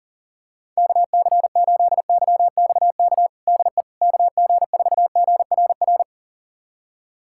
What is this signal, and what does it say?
Morse code, heard as a keyed tone, about 40 wpm, 700 Hz. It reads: KC8YXK DE KG4GRR
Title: USA Callsign DE Callsign - 6 Characters - 40wpm